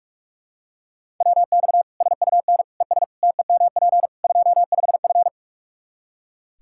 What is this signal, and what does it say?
Morse code, heard as a keyed tone, about 43 wpm, 700 Hz. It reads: WX SUN ES TEMP 25F
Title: Practice QSO - Exchange 2 - Info - 45wpm